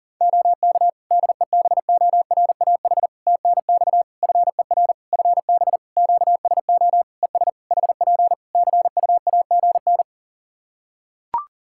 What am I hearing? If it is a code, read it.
Morse code, heard as a keyed tone, about 40 wpm, 700 Hz. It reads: OK DEBORAH TNX FER FB QSO ES HP CUAGN
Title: Practice QSO - Exchange 4 - Ending - 40wpm